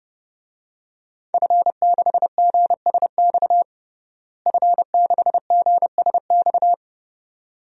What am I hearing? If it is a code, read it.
Morse code, heard as a keyed tone, about 32 wpm, 700 Hz. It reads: F6GSX F6GSX
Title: DXCC-100 Call Signs - 5 Characters - Repeated 2x - 30wpm